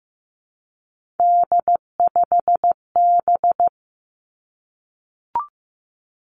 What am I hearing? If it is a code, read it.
Morse code, heard as a keyed tone, about 15 wpm, 700 Hz. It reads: D5B
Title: Three Challenging Letter-Number Combinations 15wpm